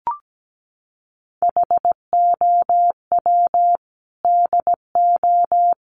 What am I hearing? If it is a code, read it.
Morse code, heard as a keyed tone, about 17 wpm, 700 Hz. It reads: HOW DO
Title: Sentences from Top 1300 Words 17wpm